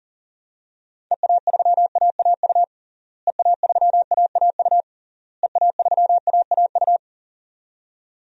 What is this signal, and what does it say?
Morse code, heard as a keyed tone, about 40 wpm, 700 Hz. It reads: EA3AAU EA3AAU EA3AAU
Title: DXCC-100 Call Signs - 6 Characters - Repeated 3x - 40wpm